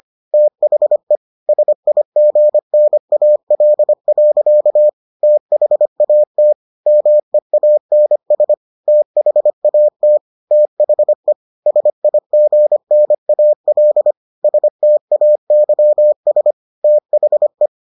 Morse code, heard as a keyed tone, about 25 wpm, 600 Hz. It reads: THE SIGNAL. THAT MEANS THAT THE SIGNAL STAYS THE